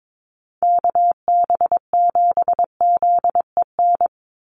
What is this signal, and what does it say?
Morse code, heard as a keyed tone, about 22 wpm, 700 Hz. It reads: KB7ZEN
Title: USA Call Signs 22wpm